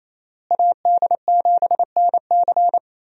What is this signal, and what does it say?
Morse code, heard as a keyed tone, about 28 wpm, 700 Hz. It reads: AD7NC
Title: USA Call Signs - 5 Characters - Repeated 3x - 28wpm